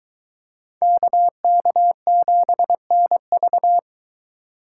Morse code, heard as a keyed tone, about 23 wpm, 700 Hz. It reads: KK7NV